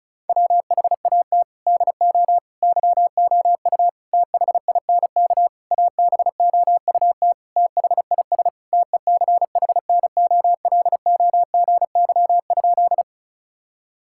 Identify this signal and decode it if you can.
Morse code, heard as a keyed tone, about 36 wpm, 700 Hz. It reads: WHAT DO YOU THINK ABOUT THIS TECHNOLOGY?
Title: Sentences from Top 1000 Words - Review Part 5 - 35wpm